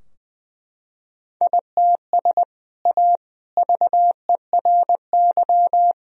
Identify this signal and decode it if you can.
Morse code, heard as a keyed tone, about 21 wpm, 700 Hz. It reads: ITS A VERY
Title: Sentences from Top 1000 Words - Review Part 3 - 20wpm